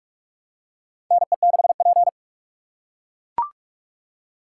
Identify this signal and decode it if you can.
Morse code, heard as a keyed tone, about 42 wpm, 700 Hz. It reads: NE6P
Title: USA Call Signs - 4 Characters - 8x Character Spacing - 45wpm